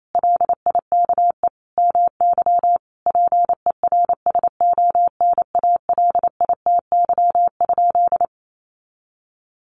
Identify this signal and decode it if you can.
Morse code, heard as a keyed tone, about 28 wpm, 700 Hz. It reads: LIKE MY PERSONALITY?
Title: Sentences from Top 1400 Words 28wpm